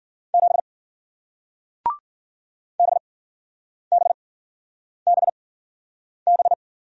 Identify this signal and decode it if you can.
Morse code, heard as a keyed tone, about 53 wpm, spaced as for 9 wpm, 700 Hz. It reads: 7 BBBB